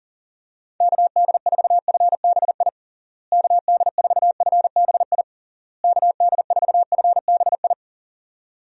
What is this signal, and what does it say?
Morse code, heard as a keyed tone, about 40 wpm, 700 Hz. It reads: KD4FBI KD4FBI KD4FBI
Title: USA Call Signs - 6 Characters - Repeated 3x - 40wpm